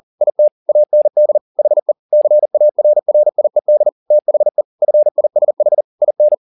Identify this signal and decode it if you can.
Morse code, heard as a keyed tone, about 40 wpm, 600 Hz. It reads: IT AND HE CARRIED THE FISH IN
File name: the old man and the sea 2 40wpm